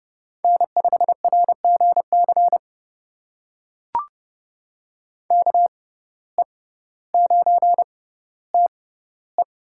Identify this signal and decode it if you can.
Morse code, heard as a keyed tone, about 30 wpm, spaced as for 15 wpm, 700 Hz. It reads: N5RGC KE9TE
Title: USA Call Signs - 5 Characters - 6x Character Spacing - 30wpm